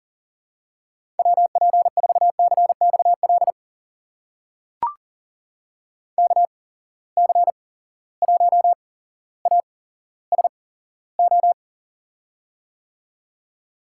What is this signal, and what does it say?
Morse code, heard as a keyed tone, about 40 wpm, spaced as for 17 wpm, 700 Hz. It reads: WP4CXL KC1ASO
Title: USA Call Signs - 6 Characters - 8x Character Spacing - 40wpm